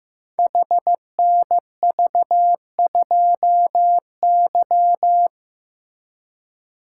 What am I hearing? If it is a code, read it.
Morse code, heard as a keyed tone, about 15 wpm, 700 Hz. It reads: HNV2Y
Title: Code Groups - 5 Letter:Numbers - 15wpm